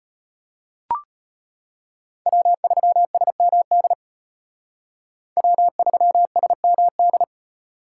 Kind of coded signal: Morse code

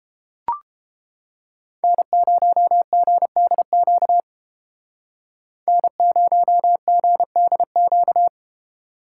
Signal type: Morse code

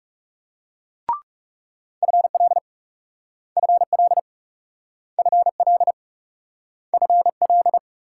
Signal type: Morse code

